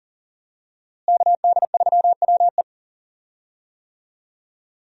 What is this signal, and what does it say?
Morse code, heard as a keyed tone, about 40 wpm, 700 Hz. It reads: KD3WE
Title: USA Call Signs - 5 Characters - Repeated 3x - 40wpm